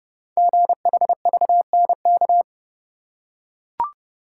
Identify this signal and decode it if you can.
Morse code, heard as a keyed tone, about 30 wpm, 690 Hz. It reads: GHVNK
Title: Code Groups - 5 Letters - 30wpm